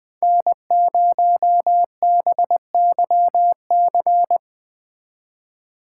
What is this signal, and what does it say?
Morse code, heard as a keyed tone, about 20 wpm, 700 Hz. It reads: N0BYC